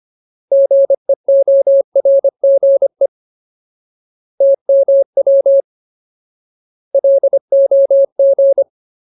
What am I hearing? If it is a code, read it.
Morse code, heard as a keyed tone, about 25 wpm, 550 Hz. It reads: GEORGE TMW LOG
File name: Ultimate Word List / 25WPM Ultimate